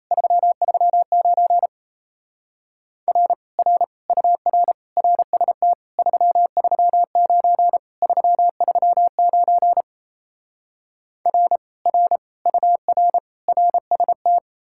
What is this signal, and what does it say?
Morse code, heard as a keyed tone, about 37 wpm, 700 Hz. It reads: 339 R R UR RST 339 339 R R UR RST